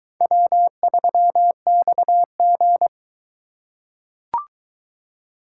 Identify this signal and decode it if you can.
Morse code, heard as a keyed tone, about 20 wpm, 700 Hz. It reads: W3XG